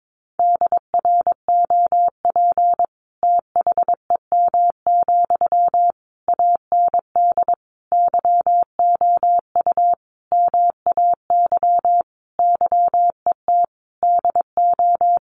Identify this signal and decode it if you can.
Morse code, heard as a keyed tone, about 22 wpm, 700 Hz. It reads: DROP THEM, AND YOU MAY YET DO
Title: Sentences from Top 1400 Words 22wpm